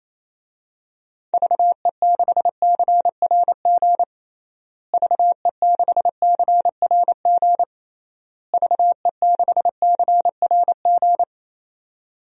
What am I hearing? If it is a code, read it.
Morse code, heard as a keyed tone, about 27 wpm, 700 Hz. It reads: VE6CRG VE6CRG VE6CRG
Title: DXCC-100 Call Signs - 6 Characters - Repeated 3x - 28wpm